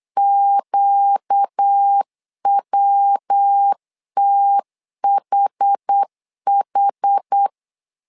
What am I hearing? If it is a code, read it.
Morse code, heard as a keyed tone, about 8 wpm, 800 Hz. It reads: QWTHH
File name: 4 Q 8wpm